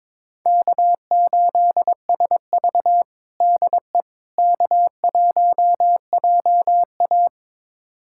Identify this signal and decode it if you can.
Morse code, heard as a keyed tone, about 22 wpm, 700 Hz. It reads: K8SV DE K1JA
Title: USA Callsign DE Callsign - 4 Characters - 22wpm